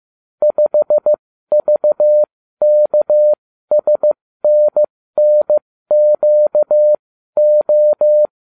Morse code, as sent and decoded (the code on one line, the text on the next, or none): ..... ...- -.- ... -. -. --.- ---
5VKSNNQO